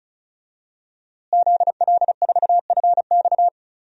Morse code, heard as a keyed tone, about 34 wpm, 700 Hz. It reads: ZL4FX
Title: DXCC-100 Call Signs - 5 Characters - Repeated 3x - 35wpm